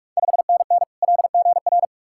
Morse code, heard as a keyed tone, about 44 wpm, 700 Hz. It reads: 5NN LGR
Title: Ontario QSO Party Exchange 45wpm